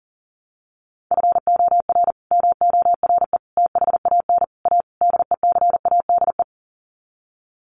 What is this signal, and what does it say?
Morse code, heard as a keyed tone, about 40 wpm, 700 Hz. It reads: FOR MORE THAN A DECADE
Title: Sets of 5 Words 40wpm